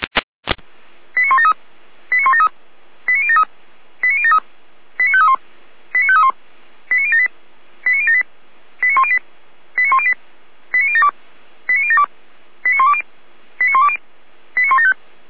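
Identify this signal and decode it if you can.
Five-tone selective call (ZVEI), heard as quick runs of five tones, tones of about 70 ms. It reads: 89173 89173 89073 89073 89531 89531 89078 89078 89108 89108 89072 89072 89120 89120 89175
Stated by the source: Five-tone ZVEI selective call received on an SDR.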